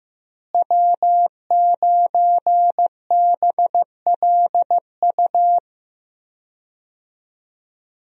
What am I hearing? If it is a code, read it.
Morse code, heard as a keyed tone, about 15 wpm, 700 Hz. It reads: W9BLU
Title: USA Call Signs - 5 Characters - Repeated 2x - 15wpm